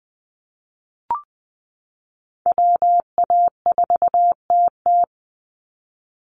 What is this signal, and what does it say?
Morse code, heard as a keyed tone, about 20 wpm, 700 Hz. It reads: WA4TT
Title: Call Signs Repeated 2x - 20wpm